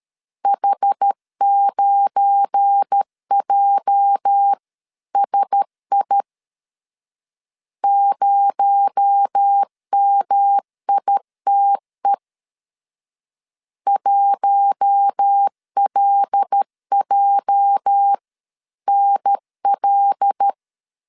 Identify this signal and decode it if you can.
Morse code, heard as a keyed tone, about 13 wpm, 800 Hz. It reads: H9JSI 0MITE 1LJNL
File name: LEZIONI / 8 YJ45 / 8 YJ 12wpm 2235